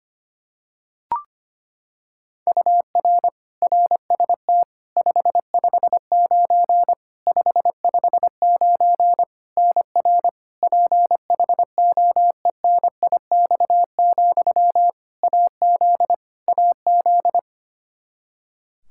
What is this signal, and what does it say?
Morse code, heard as a keyed tone, about 25 wpm, 700 Hz. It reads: UR RST 559 559 NR PHOENIX, AZ AZ
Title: Practice QSO - Exchange 1 - Beginning - 25wpm